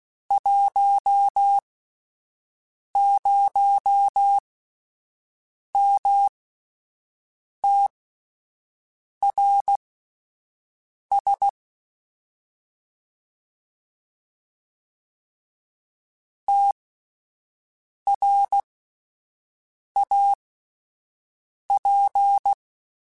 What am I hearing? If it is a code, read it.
Morse code, heard as a keyed tone, about 16 wpm, spaced as for 5 wpm, 800 Hz. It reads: 10MTRS TRAP
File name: QSOs / QSOs-mp3s / WD0FBT 1@10